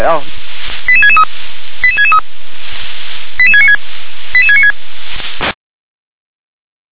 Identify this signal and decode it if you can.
Five-tone selective call (ZVEI), heard as quick runs of five tones, tones of about 70 ms, with speech. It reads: 88663 88663 88696 88696